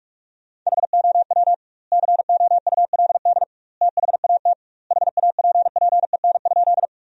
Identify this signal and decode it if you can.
Morse code, heard as a keyed tone, about 46 wpm, 700 Hz. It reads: HOW COULD THAT HAPPEN?
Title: 2x Word-Spacing - Sentences from Top 400 Words - 45wpm